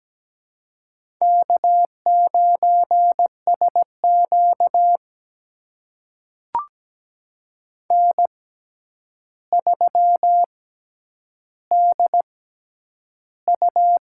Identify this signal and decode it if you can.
Morse code, heard as a keyed tone, about 17 wpm, spaced as for 9 wpm, 700 Hz. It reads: K9SQ N3DU
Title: USA Call Signs - 4 Characters - 6x Character Spacing - 17wpm